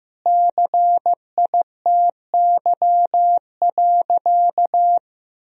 Morse code, heard as a keyed tone, about 14 wpm, 700 Hz. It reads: CITY.
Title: Sentences from Top 1000 Words - Review Part 2 - 15wpm